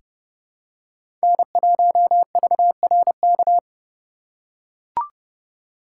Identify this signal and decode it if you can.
Morse code, heard as a keyed tone, about 30 wpm, 700 Hz. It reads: N1VRK
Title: USA Call Signs - 3x Character Spacing - 30wpm